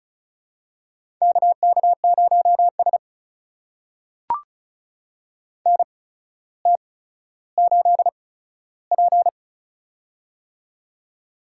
Morse code, heard as a keyed tone, about 34 wpm, spaced as for 15 wpm, 700 Hz. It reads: KK0S NT8P
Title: USA Call Signs - 4 Characters - 8x Character Spacing - 35wpm